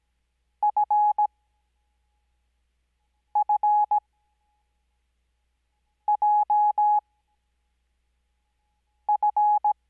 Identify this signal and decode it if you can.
Morse code, heard as a keyed tone, about 17 wpm, spaced as for 4 wpm, 840 Hz. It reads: FFJF
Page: Farnsworth Spacing: 15WPM Character speed Spaced at 5WPM.